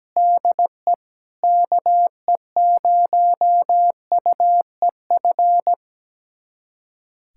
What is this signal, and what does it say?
Morse code, heard as a keyed tone, about 17 wpm, 700 Hz. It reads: DE KE0UEF
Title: USA Callsign DE Callsign - 6 Characters - 17wpm